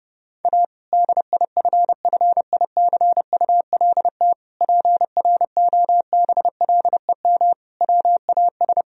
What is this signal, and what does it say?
Morse code, heard as a keyed tone, about 30 wpm, 700 Hz. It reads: A DIFFICULT PROBLEM WAS
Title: Sentences from Top 1000 Words - Review Part 2 - 30wpm